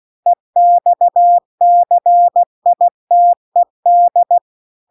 Morse code, heard as a keyed tone, about 16 wpm, 700 Hz. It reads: EXCITED
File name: 16WPM